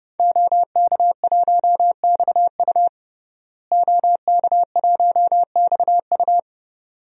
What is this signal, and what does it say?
Morse code, heard as a keyed tone, about 30 wpm, 700 Hz. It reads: OK1XU OK1XU
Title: DXCC-100 Call Signs - 5 Characters - Repeated 2x - 30wpm